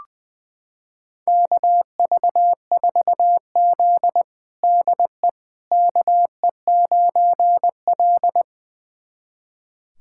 Morse code, heard as a keyed tone, about 20 wpm, 700 Hz. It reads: KV4Z DE KE9L
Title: USA Callsign DE Callsign - 4 Characters - 20wpm